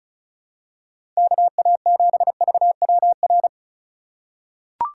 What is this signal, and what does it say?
Morse code, heard as a keyed tone, about 34 wpm, 700 Hz. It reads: KA7VWR
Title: USA Call Signs - 6 Characters - 4x Character Spacing - 35wpm